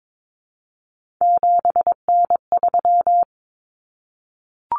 Morse code, heard as a keyed tone, about 22 wpm, 700 Hz. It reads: 7N3
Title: Common Cut Numbers - 3-Digit 22wpm